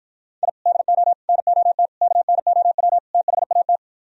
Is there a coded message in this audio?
Yes: Morse code from 0.4 to 3.8 s, heard as a keyed tone, about 52 wpm, 700 Hz. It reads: I DO NOT KNOW THAT